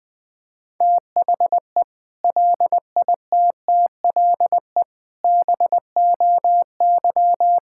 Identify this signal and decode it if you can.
Morse code, heard as a keyed tone, about 21 wpm, 700 Hz. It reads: THE LITTLE BOY
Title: Sentences from Top 1600 Words 20wpm